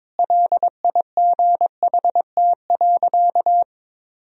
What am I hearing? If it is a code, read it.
Morse code, heard as a keyed tone, about 22 wpm, 700 Hz. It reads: LIGHT.
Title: Sentences from Top 1000 Words - Review Part 1 - 22wpm